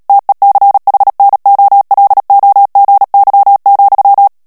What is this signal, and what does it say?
Morse code, heard as a keyed tone, about 37 wpm, 800 Hz. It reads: TECHNOLOGY,
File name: morse mp3 / ANGLAIS texte clair 3 vitesse 35 mots minute